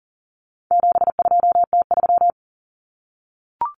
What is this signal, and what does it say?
Morse code, heard as a keyed tone, about 40 wpm, 700 Hz. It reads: 72T3
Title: Common Cut Numbers - 4-Digit 40wpm